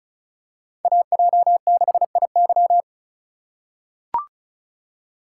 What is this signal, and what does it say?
Morse code, heard as a keyed tone, about 34 wpm, 700 Hz. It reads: AJ6IY
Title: USA Call Signs - 5 Characters - 2x Character Spacing - 35wpm